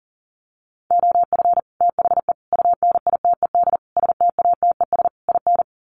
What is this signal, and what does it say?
Morse code, heard as a keyed tone, about 37 wpm, 700 Hz. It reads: OF THE UNITED STATES IN
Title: Sets of 5 Words Encore 40wpm